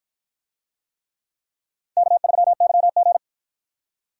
Morse code, heard as a keyed tone, about 50 wpm, 700 Hz. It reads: K3YG